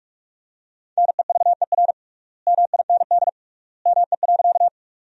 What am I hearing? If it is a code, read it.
Morse code, heard as a keyed tone, about 46 wpm, 700 Hz. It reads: NEVER MIND ME.
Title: 3x Word-Spacing - Sentences from Top 500 Words - 45wpm